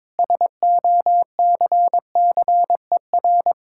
Morse code, heard as a keyed tone, about 22 wpm, 690 Hz. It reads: SOCCER
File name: practice / 22wpm / Easy Sentences with Hints - 22wpm